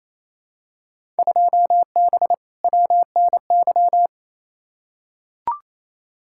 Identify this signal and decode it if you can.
Morse code, heard as a keyed tone, about 28 wpm, 700 Hz. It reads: 2B WNY
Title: Field Day Exchanges Encore - 28wpm